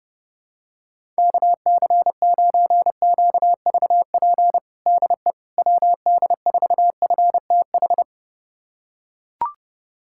Morse code, heard as a keyed tone, about 30 wpm, 700 Hz. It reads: KC9QVP DE WD4FTH
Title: USA Callsign DE Callsign - 6 Characters - 30wpm